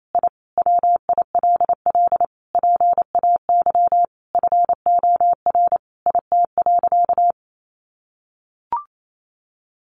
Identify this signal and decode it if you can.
Morse code, heard as a keyed tone, about 28 wpm, 700 Hz. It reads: I WILL PAY FOR IT.
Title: Sentences from Top 1300 Words 28wpm